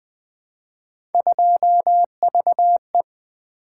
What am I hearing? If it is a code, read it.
Morse code, heard as a keyed tone, about 20 wpm, 700 Hz. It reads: 2VE
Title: Three Letter-Number Combinations - 20wpm